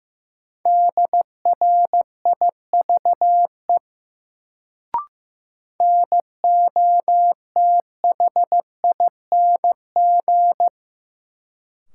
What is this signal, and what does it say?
Morse code, heard as a keyed tone, about 15 wpm, 700 Hz. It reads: DRIVE NOTHING
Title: Sentences from Top 500 Words - New Word Drill - 15wpm